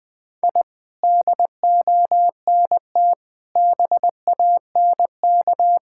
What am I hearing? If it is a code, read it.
Morse code, heard as a keyed tone, about 20 wpm, 700 Hz. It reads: I DONT BANK
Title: Sentences from Top 1000 Words - Review Part 6 - 20wpm